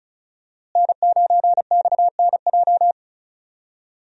Morse code, heard as a keyed tone, about 36 wpm, 700 Hz. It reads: N9XNJ
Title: USA Call Signs - 5 Characters - Repeated 3x - 35wpm